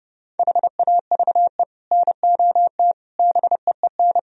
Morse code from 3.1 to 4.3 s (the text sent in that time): BEEN